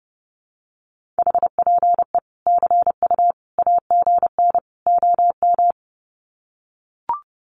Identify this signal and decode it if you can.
Morse code, heard as a keyed tone, about 30 wpm, 700 Hz. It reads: HPE CU AGN OM
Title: Common QSO Phrases 30wpm